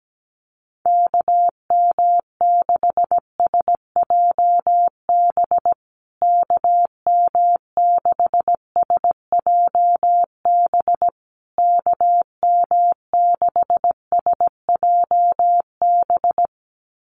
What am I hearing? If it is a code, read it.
Morse code, heard as a keyed tone, about 17 wpm, 700 Hz. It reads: KM6SJB KM6SJB KM6SJB
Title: USA Call Signs Repeated 3x 17wpm